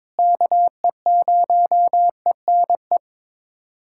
Morse code, heard as a keyed tone, about 21 wpm, 700 Hz. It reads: KE0ENE
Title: USA Callsign DE Callsign 22wpm